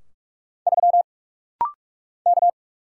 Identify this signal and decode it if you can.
Morse code, heard as a keyed tone, about 45 wpm, 700 Hz. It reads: 3 K
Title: Single Letter-Number Alt - Rapid-Fire 45wpm